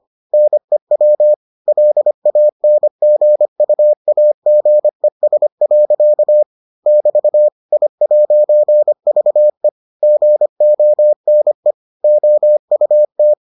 Morse code, heard as a keyed tone, about 25 wpm, 600 Hz. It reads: NEW LANGUAGES. <BT> I'VE GONE OUT